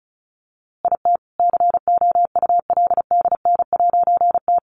Morse code, heard as a keyed tone, about 34 wpm, 700 Hz. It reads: IT COULDN'T
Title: A Sound Of Thunder 35wpm